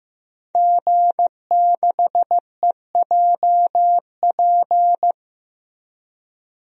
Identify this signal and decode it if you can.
Morse code, heard as a keyed tone, about 15 wpm, 700 Hz. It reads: G6EJP
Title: DXCC-100 Call Signs - 5 Characters - Repeated 2x - 15wpm